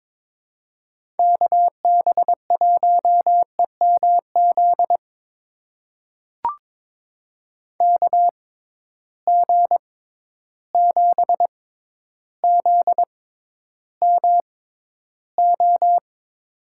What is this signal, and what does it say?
Morse code, heard as a keyed tone, about 22 wpm, spaced as for 11 wpm, 700 Hz. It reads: KB1EMZ KG7ZMO
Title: USA Call Signs - 6 Characters - 6x Character Spacing - 22wpm